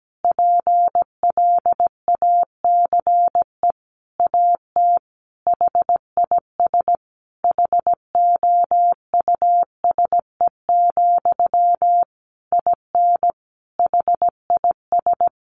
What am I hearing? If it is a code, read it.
Morse code, heard as a keyed tone, about 17 wpm, 700 Hz. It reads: PLACE AT HIS HOUSE, IN HIS
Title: Sentences from Top 300 Words 17wpm